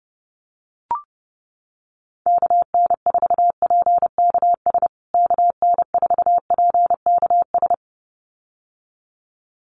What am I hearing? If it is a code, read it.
Morse code, heard as a keyed tone, about 30 wpm, 700 Hz. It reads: KN4PKS KN4PKS
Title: USA Call Signs Repeated 2x 30wpm